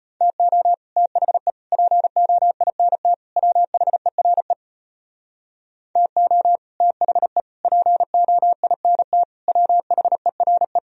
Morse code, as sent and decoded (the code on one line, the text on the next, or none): - --- / - .... / .--. --- .. -. - / .-- .... .-. / - --- / - .... / .--. --- .. -. - / .-- .... .-. .
TO THE POINT WHERE TO THE POINT WHERE